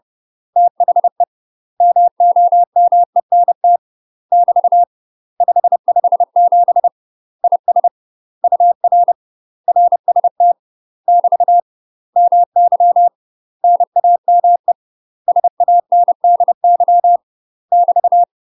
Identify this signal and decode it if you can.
Morse code, heard as a keyed tone, about 30 wpm, 700 Hz. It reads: THE MOMENT <BT> 557 IS UR RST <BT> MY NAME SANDY <BT>
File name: UK MASTER QSOs TRAINING / 30WPM / UK MASTER QSOs 0051